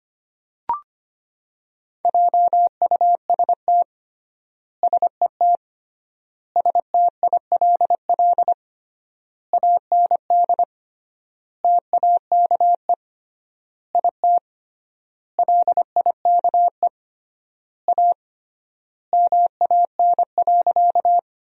Morse code, heard as a keyed tone, about 25 wpm, 700 Hz. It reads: JUST SET STILL AND TAKE IT LIKE A MAN.
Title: 3x Word-Spacing - Sentences from Top 300 Words - 25wpm